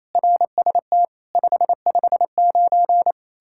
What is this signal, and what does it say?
Morse code, heard as a keyed tone, about 37 wpm, 700 Hz. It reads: RST 559